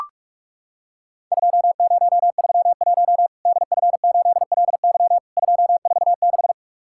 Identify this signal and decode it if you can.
Morse code, heard as a keyed tone, about 44 wpm, 700 Hz. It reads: 2031 DF8LY 246
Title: Worked All Europe DX Contest - QTC Exchange - 45wpm